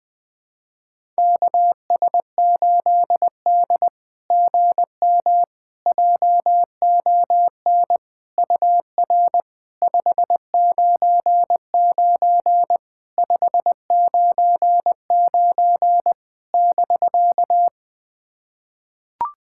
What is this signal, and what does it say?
Morse code, heard as a keyed tone, about 20 wpm, 700 Hz. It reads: KS8D GM JON UR 599 599 <BK>
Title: SOTA QSO Practice 20wpm